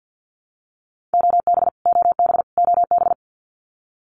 Morse code, heard as a keyed tone, about 50 wpm, 700 Hz. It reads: OB OB OB
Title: Abbreviations Repeated 3x - 50wpm